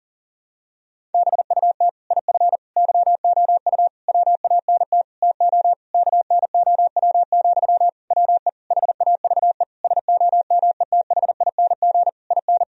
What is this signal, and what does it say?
Morse code, heard as a keyed tone, about 40 wpm, 700 Hz. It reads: BUT IF YOU WANT TO KNOW, WE HAVE SOMETHING IN
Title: Sentences from Top 1000 Words - Review Part 3 - 40wpm